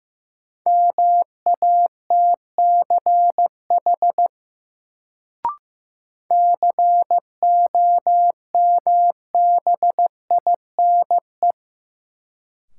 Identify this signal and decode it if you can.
Morse code, heard as a keyed tone, about 15 wpm, 700 Hz. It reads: MATCH COMBINE
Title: Sentences from Top 1600 Words - New Word Drill - 15wpm